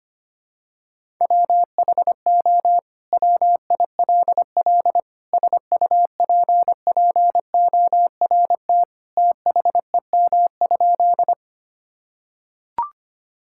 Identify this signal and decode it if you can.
Morse code, heard as a keyed tone, about 25 wpm, 700 Hz. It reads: WHO WILL SUPPORT THEM?
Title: Sentences from Top 1000 Words - Review Part 5 - 25wpm